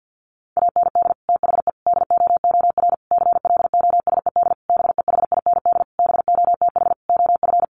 Morse code, heard as a keyed tone, about 48 wpm, 700 Hz. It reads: AND THE DOOR CLOSED BEHIND BOTH OF
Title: Sentences from Top 1400 Words 50wpm